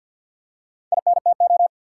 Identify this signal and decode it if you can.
Morse code, heard as a keyed tone, about 53 wpm, 700 Hz. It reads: INTO
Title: Four Letter Words - 50wpm